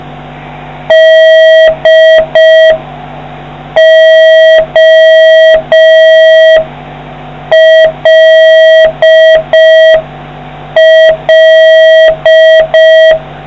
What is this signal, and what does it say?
Morse code, heard as a keyed tone, about 5 wpm, 1950 Hz, with noise. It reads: DOLL